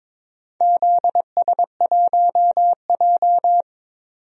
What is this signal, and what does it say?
Morse code, heard as a keyed tone, about 22 wpm, 700 Hz. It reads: ZS1J
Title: DXCC-100 Call Signs - 4 Characters - 22wpm